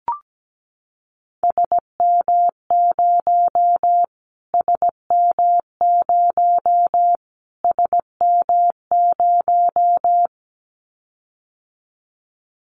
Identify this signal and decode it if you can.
Morse code, heard as a keyed tone, about 17 wpm, 700 Hz. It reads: SM0 SM0 SM0
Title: Callsign Prefixes for Top 25 DXCC - Repeated 3x - 17wpm